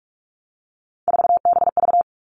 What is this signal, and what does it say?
Morse code, heard as a keyed tone, about 46 wpm, 700 Hz. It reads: 4BV
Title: Three Challenging Letter-Number Combinations 45wpm